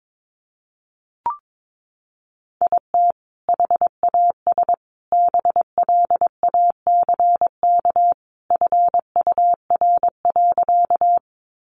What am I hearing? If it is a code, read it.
Morse code, heard as a keyed tone, about 22 wpm, 700 Hz. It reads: IT HAS BLACK FUR.
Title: Easy Phrases 22wpm